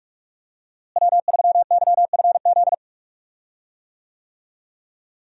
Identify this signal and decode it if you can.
Morse code, heard as a keyed tone, about 46 wpm, 700 Hz. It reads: W3YFZ
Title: USA Call Signs - 5 Characters - Repeated 3x - 45wpm